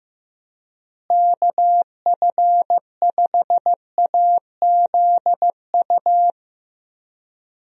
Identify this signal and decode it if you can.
Morse code, heard as a keyed tone, about 15 wpm, 700 Hz. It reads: KF5AZU
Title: USA Call Signs - 6 Characters - Repeated 2x - 15wpm